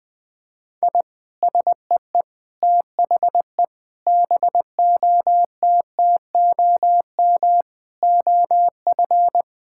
Morse code, heard as a keyed tone, about 20 wpm, 700 Hz. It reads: I SEE THE BOTTOM OF
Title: Sentences from Top 1000 Words - Review Part 6 - 20wpm